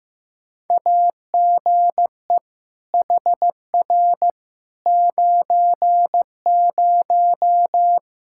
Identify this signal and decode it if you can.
Morse code, heard as a keyed tone, about 15 wpm, 700 Hz. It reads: AGE HR 90
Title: Practice QSO - Exchange 3 - More Info - 15wpm